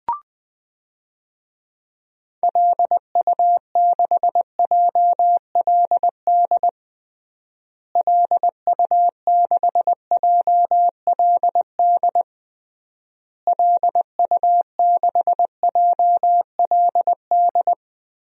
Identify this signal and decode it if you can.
Morse code, heard as a keyed tone, about 19 wpm, 700 Hz. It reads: LU6JLD LU6JLD LU6JLD
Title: DXCC-100 Call Signs - 6 Characters - Repeated 3x - 20wpm